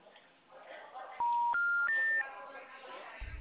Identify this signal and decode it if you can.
Special information tone, with speech.